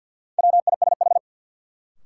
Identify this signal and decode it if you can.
Morse code, heard as a keyed tone, about 48 wpm, 700 Hz. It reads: WISH
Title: Sentences from Top 1600 Words - New Word Drill Alt - 50wpm